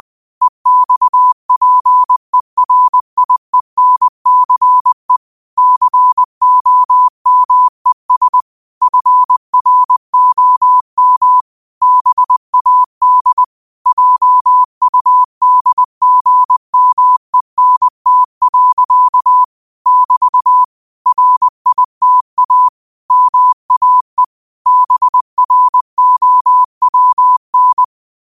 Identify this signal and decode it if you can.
Morse code, heard as a keyed tone, about 20 wpm, 1000 Hz. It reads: EXPERIENCE COMES FROM BAD JUDGMENT. <BT> RITA MAE BROWN
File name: qotdfiles / 20wpm